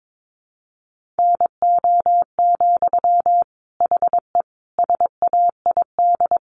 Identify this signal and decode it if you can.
Morse code, heard as a keyed tone, about 22 wpm, 700 Hz. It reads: NO, HE SAID
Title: Sentences from Top 1100 Words 22wpm